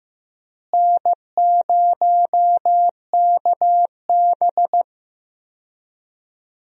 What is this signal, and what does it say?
Morse code, heard as a keyed tone, about 15 wpm, 700 Hz. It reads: N0KB
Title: USA Call Signs - 4 Characters - Repeated 3x - 15wpm